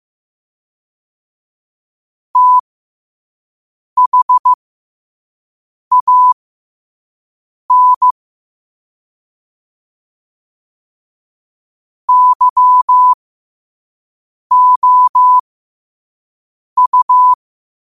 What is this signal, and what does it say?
Morse code, heard as a keyed tone, about 15 wpm, spaced as for 5 wpm, 1000 Hz. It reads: THAN YOU